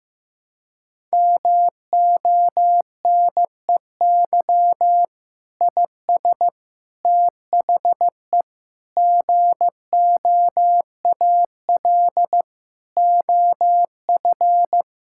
Morse code, heard as a keyed tone, about 15 wpm, 700 Hz. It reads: MONEY IS THE GOAL OF
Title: Sentences from Top 1000 Words - Review Part 5 - 15wpm